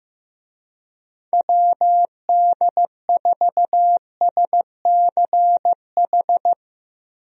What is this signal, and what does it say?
Morse code, heard as a keyed tone, about 15 wpm, 700 Hz. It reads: WD4SCH
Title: USA Call Signs - 6 Characters - Repeated 2x - 15wpm